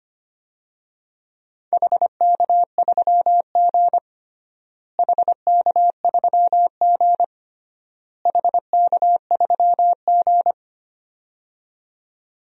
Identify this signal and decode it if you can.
Morse code, heard as a keyed tone, about 25 wpm, 700 Hz. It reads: HK3G HK3G HK3G
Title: DXCC-100 Call Signs - 4 Characters - Repeated 3x - 25wpm